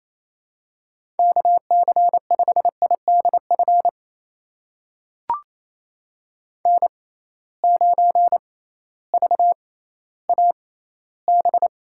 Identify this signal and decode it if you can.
Morse code, heard as a keyed tone, about 28 wpm, spaced as for 15 wpm, 700 Hz. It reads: KC5IDF N9VAB
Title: USA Call Signs - 6x Character Spacing - 28wpm